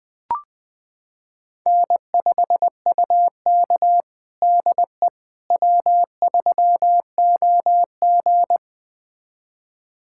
Morse code, heard as a keyed tone, about 20 wpm, 700 Hz. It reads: N5UK DE W3OG
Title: USA Callsign DE Callsign - 4 Characters - 20wpm